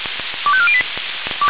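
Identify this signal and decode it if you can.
Five-tone selective call (ZVEI), heard as quick runs of five tones, tones of about 70 ms. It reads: 25408
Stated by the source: ZVEI five-tone selective call 25447 received by SDR.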